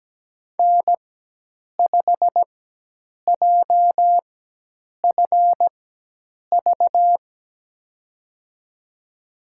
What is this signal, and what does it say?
Morse code, heard as a keyed tone, about 17 wpm, spaced as for 8 wpm, 700 Hz. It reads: N5JFV
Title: USA Call Signs - 5 Characters - 4x Character Spacing - 17wpm